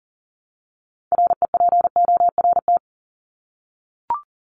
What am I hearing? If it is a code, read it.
Morse code, heard as a keyed tone, about 40 wpm, 700 Hz. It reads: REPORT
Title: Top 751-1000 Words 40wpm